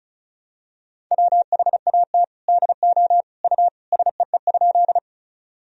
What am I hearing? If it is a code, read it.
Morse code, heard as a keyed tone, about 34 wpm, 700 Hz. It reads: WHAT DO U SEE?
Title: Sentences from Top 100 Words and Abbreviations 35wpm